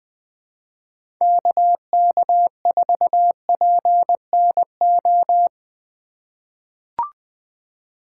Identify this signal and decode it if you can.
Morse code, heard as a keyed tone, about 20 wpm, 700 Hz. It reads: KK4PNO
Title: USA Call Signs - 6 Characters - 3x Character Spacing - 20wpm